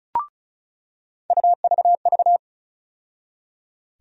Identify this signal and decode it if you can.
Morse code, heard as a keyed tone, about 34 wpm, 700 Hz. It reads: UVV
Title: USA Call Sign Postfixes - 3 Letters 35wpm